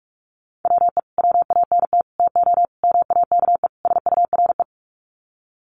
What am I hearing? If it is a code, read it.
Morse code, heard as a keyed tone, about 44 wpm, 700 Hz. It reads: WE WANT TO MAKE SURE
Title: Sets of 5 Words Encore2 45wpm